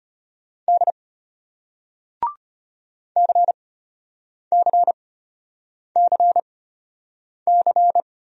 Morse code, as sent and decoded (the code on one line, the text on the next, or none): -.. / -.-. -.-. -.-. -.-.
D CCCC